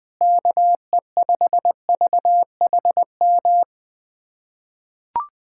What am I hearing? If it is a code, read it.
Morse code, heard as a keyed tone, about 20 wpm, 700 Hz. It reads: KE5VHM
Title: USA Callsign DE Callsign 20wpm